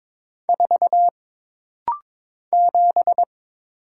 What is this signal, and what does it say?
Morse code, heard as a keyed tone, about 22 wpm, 700 Hz. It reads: 4 7
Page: This rapid-fire timing is geared to push your ICR (Instant Character Recognition) skills.